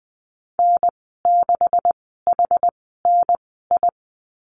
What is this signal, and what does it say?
Morse code, heard as a keyed tone, about 20 wpm, spaced as for 15 wpm, 700 Hz. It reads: N6HNI
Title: 154 Lesson 25g - Letter p - Call Signs 20wpm